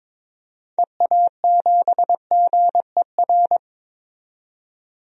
Morse code, heard as a keyed tone, about 22 wpm, 700 Hz. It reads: EA7GER
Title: DXCC-100 Call Signs - 6 Characters - 22wpm